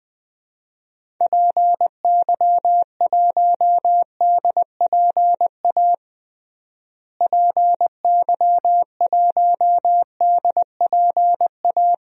Morse code, heard as a keyed tone, about 20 wpm, 700 Hz. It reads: PY1DPA PY1DPA
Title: DXCC-100 Call Signs - 6 Characters - Repeated 2x - 20wpm